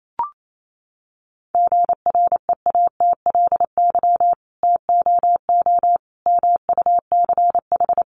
Morse code, heard as a keyed tone, about 28 wpm, 700 Hz. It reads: GREATLY TOO MUCH
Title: Sentences from Top 1400 Words 28wpm